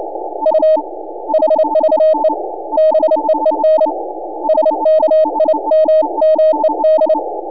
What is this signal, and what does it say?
Morse code, heard as a keyed tone, about 29 wpm, 610 Hz, with noise. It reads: U HVE BEEN SKIMMED
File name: n6tr / N6AA Skimmed